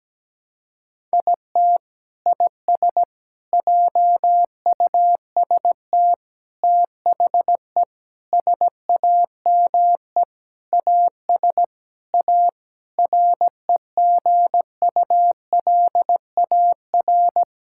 Morse code, heard as a keyed tone, about 17 wpm, 700 Hz. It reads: IT IS JUST THE SAME AS A REGULAR
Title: Sentences from Top 1000 Words - Review Part 6 - 17wpm